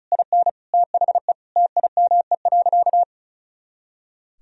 Morse code, heard as a keyed tone, about 36 wpm, 700 Hz. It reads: IN THE TIME.
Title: Sentences from Top 200 Words and Abbreviations 35wpm